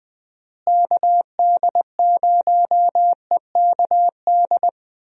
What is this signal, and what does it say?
Morse code, heard as a keyed tone, about 20 wpm, 700 Hz. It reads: KD0EKD
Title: USA Call Signs - 6 Characters - Repeated 3x - 20wpm